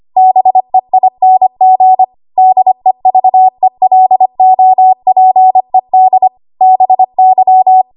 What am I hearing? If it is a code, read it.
Morse code, heard as a keyed tone, about 25 wpm, 750 Hz. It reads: BEING DEVELOPED BY
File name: Archive / 25 WPM / 25WPM